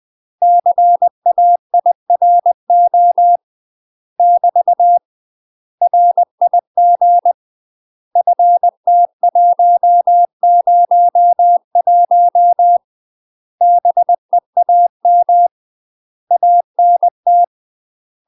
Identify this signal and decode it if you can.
Morse code, heard as a keyed tone, about 20 wpm, 700 Hz. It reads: CAIRO <BT> RIG FT101 BEAM ANT
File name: UK MASTER QSOs TRAINING / 20WPM / UK MASTER QSOs 0058